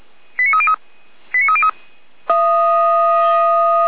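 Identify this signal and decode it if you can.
Five-tone selective call (ZVEI), heard as quick runs of five tones, tones of about 70 ms. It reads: 89303 89303
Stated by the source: Five-tone ZVEI selective call received by SDR, from fire brigade Feuerwehr Orlamünde's audio archive.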